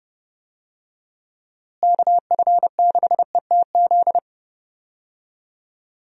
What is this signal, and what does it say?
Morse code, heard as a keyed tone, about 32 wpm, 700 Hz. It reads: KF6ETZ
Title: USA Call Signs - 6 Characters - Repeated 3x - 30wpm